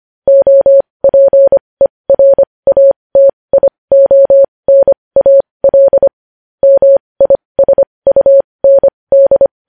Morse code, heard as a keyed tone, about 25 wpm, spaced as for 20 wpm, 550 Hz. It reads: OPERATIONAL MISUND
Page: news-headlines-in-morse-code-at-20-wpm